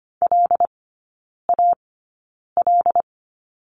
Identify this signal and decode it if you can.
Morse code, heard as a keyed tone, about 25 wpm, spaced as for 9 wpm, 700 Hz. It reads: LAL
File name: course / 25:10wpm / 093 Lesson 16e - Letter c - Three Characters 25wpm